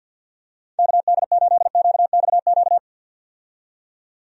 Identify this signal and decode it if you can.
Morse code, heard as a keyed tone, about 52 wpm, 700 Hz. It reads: KD8QXQ
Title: USA Call Signs - 6 Characters - 50wpm